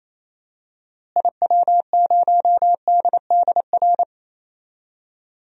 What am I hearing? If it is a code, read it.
Morse code, heard as a keyed tone, about 28 wpm, 700 Hz. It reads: IW0DDR
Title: DXCC-100 Call Signs - 6 Characters - Repeated 3x - 28wpm